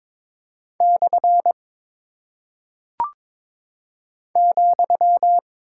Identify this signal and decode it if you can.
Morse code, heard as a keyed tone, about 22 wpm, 700 Hz. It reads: / ,
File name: practice / 22wpm / Single Punctuation-Prosign - 22wpm